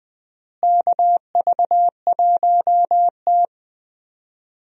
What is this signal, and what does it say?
Morse code, heard as a keyed tone, about 20 wpm, 700 Hz. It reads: KV1T
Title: USA Call Signs - 4 Characters - Repeated 3x - 20wpm